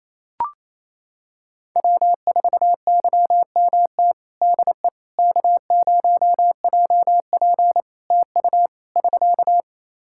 Morse code, heard as a keyed tone, about 27 wpm, 700 Hz. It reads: W4YMT DE K0JP TU <SK>
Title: Practice QSO - Exchange 4 - Ending - 28wpm